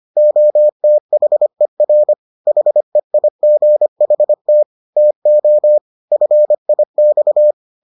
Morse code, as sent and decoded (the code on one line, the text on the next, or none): --- - .... .-. / .... .. --. .... - / - --- / ..-. .. -..-
OTHER HEIGHT TO FIX